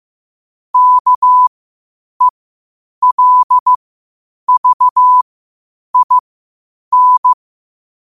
Morse code, heard as a keyed tone, about 15 wpm, spaced as for 8 wpm, 1000 Hz. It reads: KELVIN
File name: qotdfiles / 8wpm